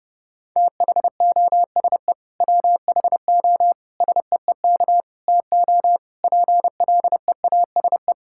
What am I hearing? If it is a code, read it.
Morse code, heard as a keyed tone, about 30 wpm, 700 Hz. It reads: THOSE WHO SEEK TO PLEASE
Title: Sentences from Top 1600 Words 30wpm